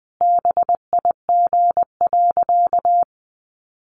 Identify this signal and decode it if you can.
Morse code, heard as a keyed tone, about 20 wpm, 700 Hz. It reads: BIG.
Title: Easy Phrases 20wpm